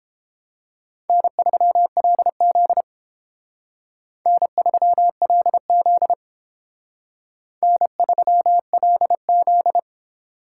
Morse code, heard as a keyed tone, about 31 wpm, 700 Hz. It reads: N3LZ N3LZ N3LZ